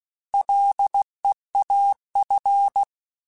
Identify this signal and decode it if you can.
Morse code, heard as a keyed tone, about 16 wpm, 800 Hz. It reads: LEAF
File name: Combinations / Combinations-mp3s / Combinations@15